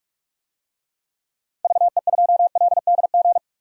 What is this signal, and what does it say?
Morse code, heard as a keyed tone, about 47 wpm, 700 Hz. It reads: VE2LDG
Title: DXCC-100 Call Signs - 6 Characters - 45wpm